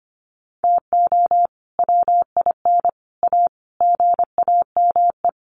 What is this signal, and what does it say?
Morse code, heard as a keyed tone, about 25 wpm, 700 Hz. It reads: TO WIN A GAME
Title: Sentences from Top 1500 Words 25wpm